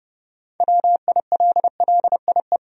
Morse code, heard as a keyed tone, about 30 wpm, 700 Hz. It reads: WILLIE
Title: USA Common Names - 30wpm